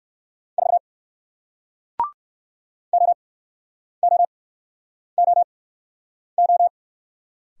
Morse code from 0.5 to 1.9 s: 4

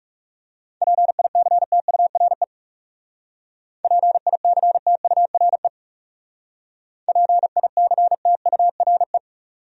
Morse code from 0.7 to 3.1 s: PICTURE

